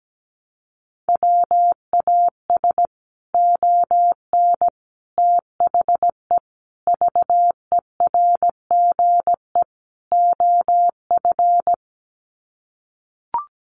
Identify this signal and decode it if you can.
Morse code, heard as a keyed tone, about 17 wpm, 700 Hz. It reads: WAS ON THE VERGE OF
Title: Sets of 5 Words Encore 17wpm